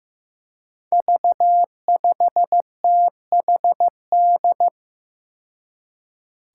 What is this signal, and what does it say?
Morse code, heard as a keyed tone, about 15 wpm, 700 Hz. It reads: V5THD
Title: Five Challenging Letter-Number Combinations 15wpm